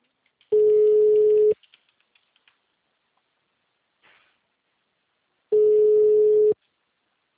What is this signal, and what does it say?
Ringback tone (Europe, 425 Hz).
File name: out- -2025- - - .16